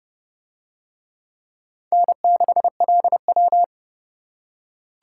Morse code, heard as a keyed tone, about 28 wpm, 700 Hz. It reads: N6LW